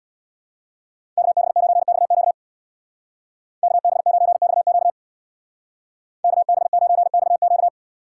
Morse code, heard as a keyed tone, about 63 wpm, 700 Hz. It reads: KB9XQ KB9XQ KB9XQ